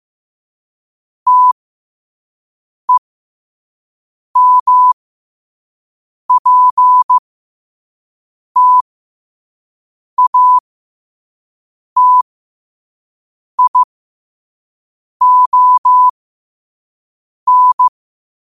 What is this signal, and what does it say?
Morse code, heard as a keyed tone, about 15 wpm, spaced as for 5 wpm, 1000 Hz. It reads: TEMPTATION
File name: qotdfiles / 5wpm